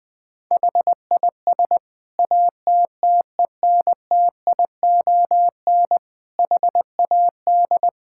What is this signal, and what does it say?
Morse code, heard as a keyed tone, about 20 wpm, 700 Hz. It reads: HIS ATTENTION HAD
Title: Sentences from Top 1000 Words - Review Part 6 - 20wpm